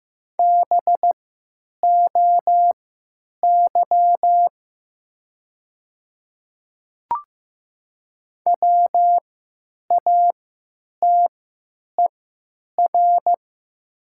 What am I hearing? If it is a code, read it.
Morse code, heard as a keyed tone, about 15 wpm, spaced as for 9 wpm, 700 Hz. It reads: BOY WATER
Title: 3x Character Spacing - Top 200 Words - 15wpm